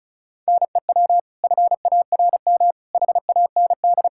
Morse code, heard as a keyed tone, about 36 wpm, 700 Hz. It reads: NEW FARM HAND
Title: Pangram Challenge v2 - 35wpm